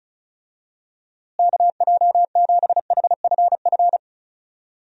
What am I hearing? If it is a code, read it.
Morse code, heard as a keyed tone, about 36 wpm, 700 Hz. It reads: KJ7HFF
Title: USA Call Signs - 6 Characters - Repeated 3x - 35wpm